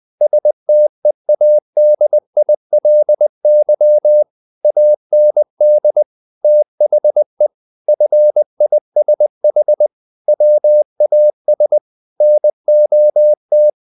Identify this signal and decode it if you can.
Morse code, heard as a keyed tone, about 20 wpm, 600 Hz. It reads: STEADILY AND THE FISH WAS NOT
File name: the old man and the sea / the old man and the sea part 1 20wpm